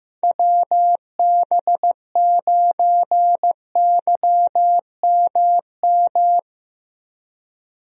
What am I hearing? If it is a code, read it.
Morse code, heard as a keyed tone, about 15 wpm, 700 Hz. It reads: WB9YMM
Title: USA Call Signs - 6 Characters - 15wpm